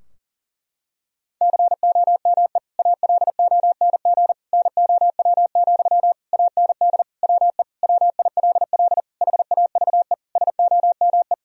Morse code, heard as a keyed tone, about 37 wpm, 700 Hz. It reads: COME ALONG NOW, AND WE WILL HAVE SOME
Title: Sentences from Top 1000 Words - Review Part 4 - 40wpm